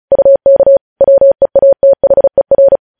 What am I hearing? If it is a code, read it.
Morse code, heard as a keyed tone, about 36 wpm, 550 Hz. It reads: UK WEATHER
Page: news headlines in morse code at 35 wpm